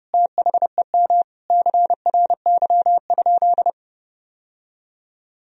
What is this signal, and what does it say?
Morse code, heard as a keyed tone, about 30 wpm, 700 Hz. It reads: THEM CRY?
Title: Sentences from Top 1000 Words - Review Part 2 - 30wpm